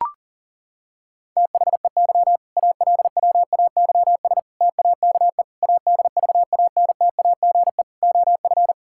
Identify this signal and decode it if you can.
Morse code, heard as a keyed tone, about 40 wpm, 700 Hz. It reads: THEY ALWAYS TAKE ADVANTAGE OF
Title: Sentences from Top 1000 Words - Review Part 6 - 40wpm